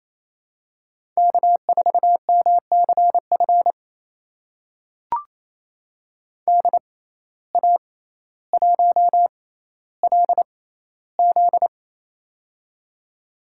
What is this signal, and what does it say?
Morse code, heard as a keyed tone, about 28 wpm, spaced as for 14 wpm, 700 Hz. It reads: K4MCF DA1LZ
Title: USA Call Signs - 5 Characters - 6x Character Spacing - 28wpm